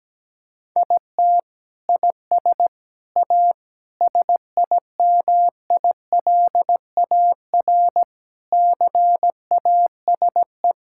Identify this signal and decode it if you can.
Morse code, heard as a keyed tone, about 17 wpm, 700 Hz. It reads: IT IS A SIMILAR CASE
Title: Sentences from Top 1000 Words - Review Part 5 - 17wpm